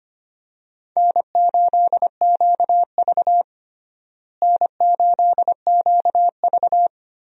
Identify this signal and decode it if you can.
Morse code, heard as a keyed tone, about 25 wpm, 700 Hz. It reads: N8QV N8QV
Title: USA Call Signs - 4 Characters - Repeated 2x - 25wpm